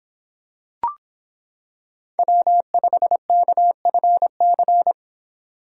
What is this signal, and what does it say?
Morse code, heard as a keyed tone, about 26 wpm, 700 Hz. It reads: W5KFC